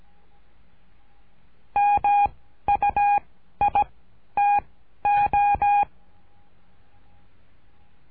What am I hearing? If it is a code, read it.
Morse code, heard as a keyed tone, about 17 wpm, 820 Hz. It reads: M U I T O
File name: cursos / CURSO DE TELEGRAFIA / Telegrafia - Trilha 13